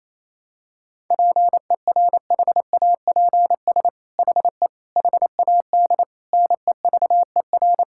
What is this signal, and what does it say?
Morse code, heard as a keyed tone, about 27 wpm, 700 Hz. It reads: PERHAPS HE HAD NEVER
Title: Sentences from Top 1600 Words 28wpm